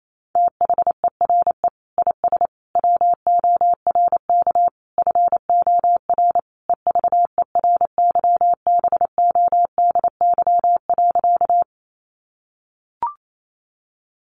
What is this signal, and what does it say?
Morse code, heard as a keyed tone, about 28 wpm, 700 Hz. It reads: THERE IS WORK FOR EVERYBODY.
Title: Sentences from Top 1400 Words 28wpm